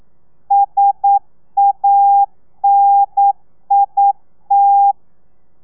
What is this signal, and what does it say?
Morse code, heard as a keyed tone, about 9 wpm, 800 Hz. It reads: SANIT
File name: telegrafia / i2tfj / Lez2 40